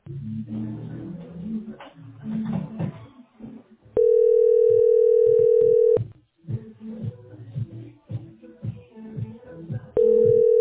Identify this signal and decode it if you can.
Ringback tone (North America), 440 + 480 Hz.